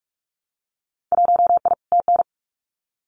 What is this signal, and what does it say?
Morse code, heard as a keyed tone, about 44 wpm, 700 Hz. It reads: 1I TN
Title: Winter Field Day Exchanges Encore 45wpm